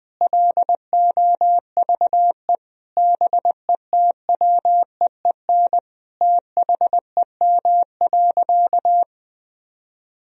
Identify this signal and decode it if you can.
Morse code, heard as a keyed tone, about 20 wpm, 700 Hz. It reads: LOVE BETWEEN THEM.
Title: Sentences from Top 1000 Words - Review Part 1 - 20wpm